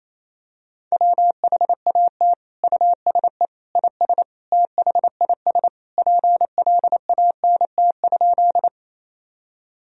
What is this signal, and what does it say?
Morse code, heard as a keyed tone, about 28 wpm, 700 Hz. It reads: WHAT USE IS THIS PLANT?
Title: Sentences from Top 1000 Words - Review Part 1 - 28wpm